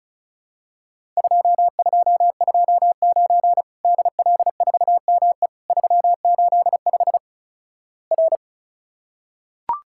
Morse code, heard as a keyed tone, about 36 wpm, 700 Hz. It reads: 2229 DL4ME 385 R
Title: Worked All Europe DX Contest - QTC Exchange - 35wpm